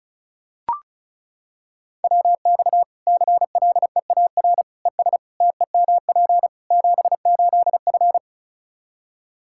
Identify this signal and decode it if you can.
Morse code, heard as a keyed tone, about 36 wpm, 700 Hz. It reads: WX CLEAR ES TEMP 78F
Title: K1USN SST QSO Practice 35wpm